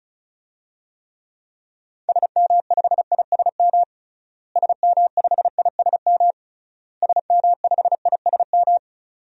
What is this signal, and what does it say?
Morse code, heard as a keyed tone, about 36 wpm, 700 Hz. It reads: SM5ISM SM5ISM SM5ISM
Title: DXCC-100 Call Signs - 6 Characters - Repeated 3x - 35wpm